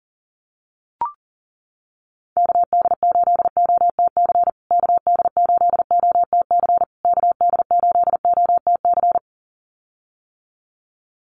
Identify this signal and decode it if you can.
Morse code, heard as a keyed tone, about 40 wpm, 700 Hz. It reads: KD8OTC KD8OTC KD8OTC
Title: USA Call Signs Repeated 3x 40wpm